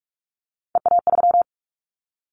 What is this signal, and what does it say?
Morse code, heard as a keyed tone, about 46 wpm, 700 Hz. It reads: EA3
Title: Callsign Prefixes for Top 25 DXCC - 45wpm